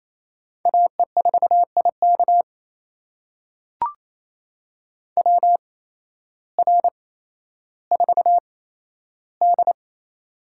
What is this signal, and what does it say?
Morse code, heard as a keyed tone, about 28 wpm, spaced as for 13 wpm, 700 Hz. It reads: AE4IK WR4D
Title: USA Call Signs - 8x Character Spacing - 28wpm